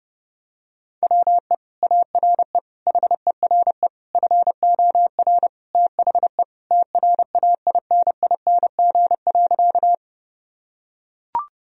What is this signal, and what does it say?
Morse code, heard as a keyed tone, about 30 wpm, 700 Hz. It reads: WE ARE HERE FOR THE TRAINING.
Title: Sentences from Top 1000 Words - Review Part 4 - 30wpm